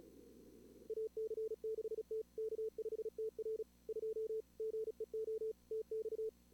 Morse code, heard as a keyed tone, about 36 wpm, 440 Hz. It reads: AC6T K5TR 2 GEO TX
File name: CWSprint / Feb2022 / k5tr-feb-na-sprint-cw-k5ot -2300